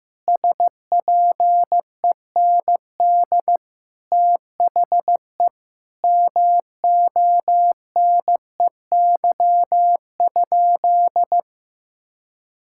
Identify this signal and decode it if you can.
Morse code, heard as a keyed tone, about 14 wpm, 700 Hz. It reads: SPEND THE MONEY?
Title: Sentences from Top 1000 Words - Review Part 5 - 15wpm